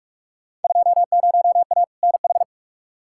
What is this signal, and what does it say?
Morse code, heard as a keyed tone, about 45 wpm, 700 Hz. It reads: 20A NH
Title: Field Day Exchanges Encore - 45wpm